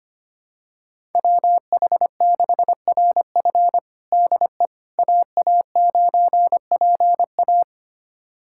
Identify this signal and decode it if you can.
Morse code, heard as a keyed tone, about 25 wpm, 700 Hz. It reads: WH6RF DE AA9PA
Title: USA Callsign DE Callsign - 5 Characters - 25wpm